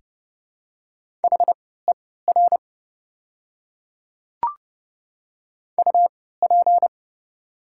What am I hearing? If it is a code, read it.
Morse code, heard as a keyed tone, about 30 wpm, spaced as for 17 wpm, 700 Hz. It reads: HER UP
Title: 3x Character Spacing - Top 300 Words - 30wpm